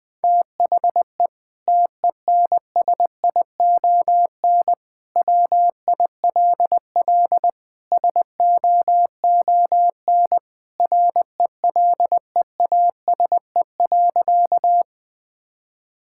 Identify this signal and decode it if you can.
Morse code, heard as a keyed tone, about 21 wpm, 700 Hz. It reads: THE TENSION WILL SOON RELEASE.
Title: Sentences from Top 1600 Words 20wpm